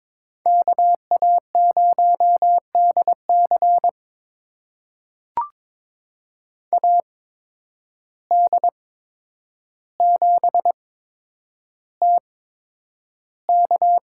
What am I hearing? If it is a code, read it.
Morse code, heard as a keyed tone, about 21 wpm, spaced as for 9 wpm, 700 Hz. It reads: KA0DC AD7TK
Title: USA Call Signs - 5 Characters - 8x Character Spacing - 22wpm